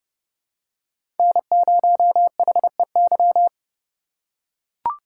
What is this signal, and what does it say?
Morse code, heard as a keyed tone, about 30 wpm, 700 Hz. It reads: N0HEY
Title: USA Call Signs - 5 Characters - 3x Character Spacing - 30wpm